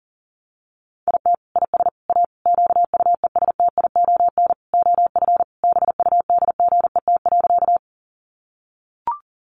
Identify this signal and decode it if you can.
Morse code, heard as a keyed tone, about 40 wpm, 700 Hz. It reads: IT IS A QUESTION OF BUDGET.
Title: Sentences from Top 1200 Words 40wpm